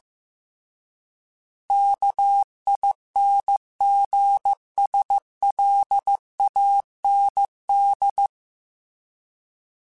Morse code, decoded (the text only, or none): KINGSLAND